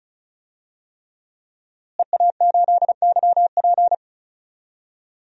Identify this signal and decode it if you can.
Morse code, heard as a keyed tone, about 34 wpm, 700 Hz. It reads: EA8YP
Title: DXCC-100 Call Signs - 5 Characters - Repeated 3x - 35wpm